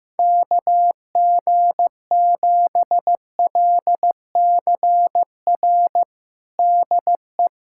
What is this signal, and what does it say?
Morse code, heard as a keyed tone, about 15 wpm, 700 Hz. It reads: KG7LCR DE
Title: USA Callsign DE Callsign 15wpm